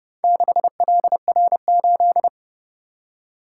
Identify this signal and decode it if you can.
Morse code, heard as a keyed tone, about 31 wpm, 700 Hz. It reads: 6LR8